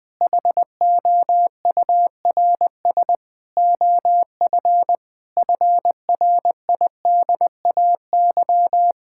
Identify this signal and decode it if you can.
Morse code, heard as a keyed tone, about 21 wpm, 700 Hz. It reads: HOURS OF FRIDAY
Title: Sentences with 6-Word Phrases - 20wpm